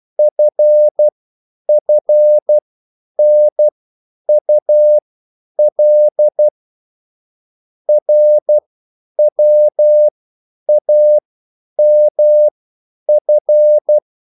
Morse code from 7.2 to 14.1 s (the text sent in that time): RWAMF